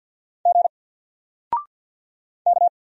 Morse code, heard as a keyed tone, about 48 wpm, 700 Hz. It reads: G K
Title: Single Letter-Number Alt 50wpm